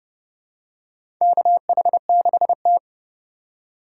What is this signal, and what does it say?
Morse code, heard as a keyed tone, about 30 wpm, 700 Hz. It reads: KH6T
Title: USA Call Signs - 4 Characters - 3x Character Spacing - 30wpm